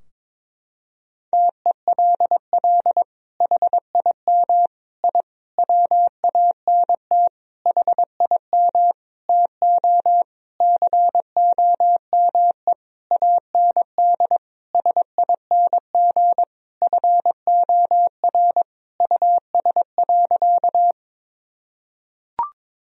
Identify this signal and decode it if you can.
Morse code, heard as a keyed tone, about 22 wpm, 700 Hz. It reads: TELL HIM I WANT HIM TO COME AND SING FOR US.
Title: Sentences from Top 1000 Words - Review Part 2 - 22wpm